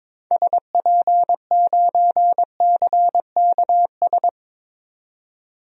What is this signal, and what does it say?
Morse code, heard as a keyed tone, about 22 wpm, 700 Hz. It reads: SP9CKS
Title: DXCC-100 Call Signs - 6 Characters - 22wpm